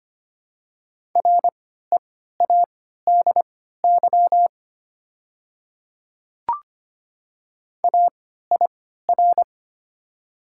Morse code, heard as a keyed tone, about 25 wpm, spaced as for 14 wpm, 700 Hz. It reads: READY AIR
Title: 3x Character Spacing - Top 500 Words - 25wpm